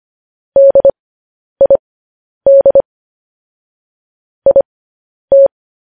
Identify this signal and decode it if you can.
Morse code, heard as a keyed tone, about 25 wpm, spaced as for 10 wpm, 550 Hz. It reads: DID IT
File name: podcast / News headlines-1x-25x10